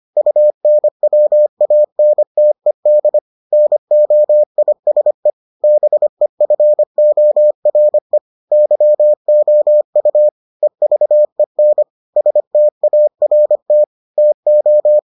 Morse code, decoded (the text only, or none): UNWANTED NOISE BEFORE YOU EVEN START TO